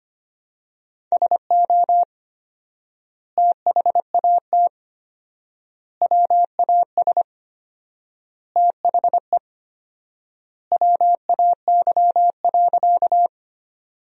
Morse code, heard as a keyed tone, about 25 wpm, 700 Hz. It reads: SO THAT WAS THE WAY.
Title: 4x Word-Spacing - Sentences from Top 100 Words - 25wpm